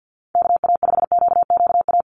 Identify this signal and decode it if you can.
Morse code, heard as a keyed tone, about 50 wpm, 700 Hz. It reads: KA5QQA
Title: NA SOTA Chaser Callsigns 50wpm